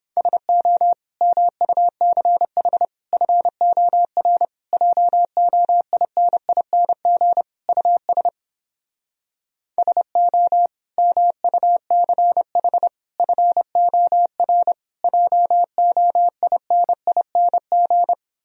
Morse code, heard as a keyed tone, about 29 wpm, 700 Hz. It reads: SO MUCH FOR JOINING US SO MUCH FOR JOINING